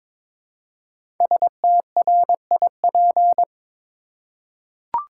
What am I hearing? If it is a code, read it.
Morse code, heard as a keyed tone, about 22 wpm, 700 Hz. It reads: STRIP
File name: 5 Letter Words - First 3 Letters Given - 22wpm